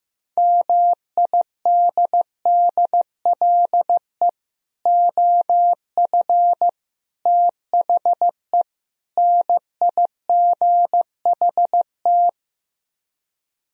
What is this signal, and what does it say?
Morse code, heard as a keyed tone, about 15 wpm, 700 Hz. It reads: MIDDLE OF THE NIGHT
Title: Speed-Racing - Sets of 4 Words 15wpm